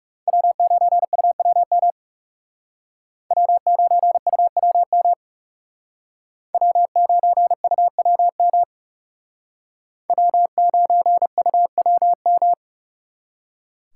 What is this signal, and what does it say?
Morse code, heard as a keyed tone, about 38 wpm, 700 Hz. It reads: W9UWM W9UWM W9UWM W9UWM